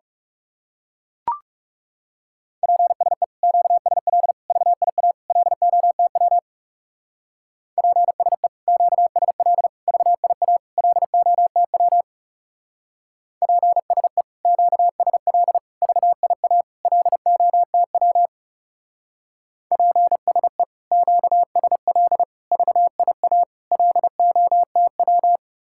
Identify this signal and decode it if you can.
Morse code, heard as a keyed tone, about 37 wpm, 700 Hz. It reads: PSE QSL VIA LOTW PSE QSL VIA LOTW PSE QSL VIA LOTW PSE QSL VIA LOTW
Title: Speed-Racing Common QSO Phrases - 30wpm